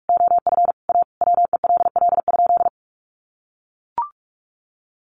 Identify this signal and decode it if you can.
Morse code, heard as a keyed tone, about 44 wpm, 700 Hz. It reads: OF A WELL?
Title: Sentences from Top 900 Words 45wpm